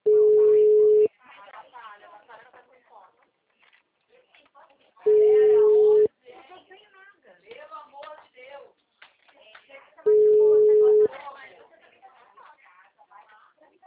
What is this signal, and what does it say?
Ringback tone (Europe, 425 Hz), with speech.